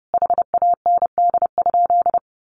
Morse code, heard as a keyed tone, about 30 wpm, 700 Hz. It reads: HAND?
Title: Sentences from Top 200 Words 30wpm